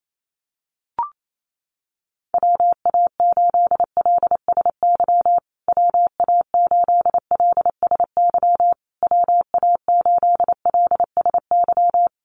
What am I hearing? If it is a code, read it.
Morse code, heard as a keyed tone, about 28 wpm, 700 Hz. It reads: WA8LSY WA8LSY WA8LSY
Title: USA Call Signs Repeated 3x 28wpm